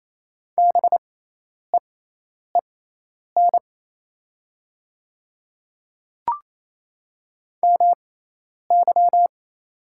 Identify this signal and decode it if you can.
Morse code, heard as a keyed tone, about 28 wpm, spaced as for 10 wpm, 700 Hz. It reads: BEEN MY
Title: 6x Character Spacing - Top 300 Words - 28wpm